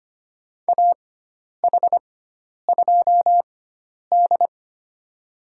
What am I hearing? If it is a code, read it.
Morse code, heard as a keyed tone, about 25 wpm, spaced as for 10 wpm, 700 Hz. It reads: AH2D
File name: Lesson 16g - Letter c - Call Signs 25wpm